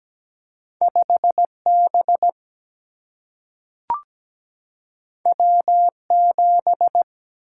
Morse code, heard as a keyed tone, about 17 wpm, 700 Hz. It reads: 5B W7
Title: Single-Letter Single-Number 17wpm